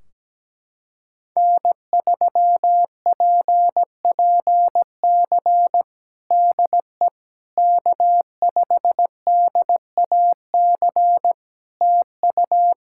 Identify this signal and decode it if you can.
Morse code, heard as a keyed tone, about 17 wpm, 700 Hz. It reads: N3PPC DE K5DAC TU
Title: K1USN SST QSO Practice 17wpm